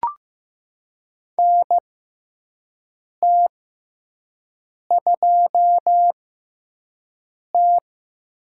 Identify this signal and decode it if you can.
Morse code, heard as a keyed tone, about 15 wpm, spaced as for 5 wpm, 700 Hz. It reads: NT2T
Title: USA Call Signs - 4 Characters - 6x Character Spacing - 15wpm